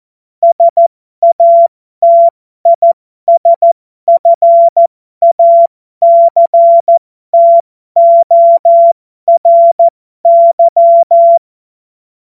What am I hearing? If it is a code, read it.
Morse code, heard as a keyed tone, about 14 wpm, spaced as for 12 wpm, 680 Hz. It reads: SATISFACTORY
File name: Morsekurs / Witze